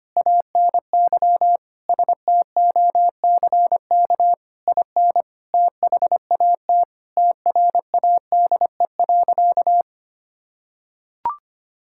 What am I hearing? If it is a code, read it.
Morse code, heard as a keyed tone, about 25 wpm, 700 Hz. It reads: ANY STOCK IN THAT TRADE.
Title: Sentences from Top 1000 Words - Review Part 5 - 25wpm